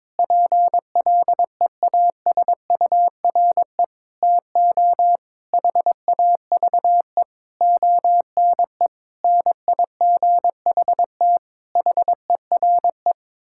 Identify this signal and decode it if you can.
Morse code, heard as a keyed tone, about 22 wpm, 700 Hz. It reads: PLEASURE TO HAVE ONE NIGHT HERE
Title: Sentences from Top 1600 Words 22wpm